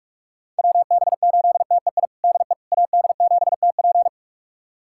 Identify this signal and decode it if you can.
Morse code, heard as a keyed tone, about 45 wpm, 700 Hz. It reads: WB8TEI DE AD7TP
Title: USA Callsign DE Callsign 45wpm